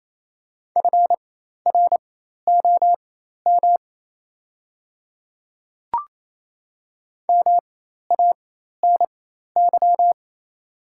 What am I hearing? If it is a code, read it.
Morse code, heard as a keyed tone, about 27 wpm, spaced as for 13 wpm, 700 Hz. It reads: FROM MANY
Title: 4x Character Spacing - Top 100 Words - 28wpm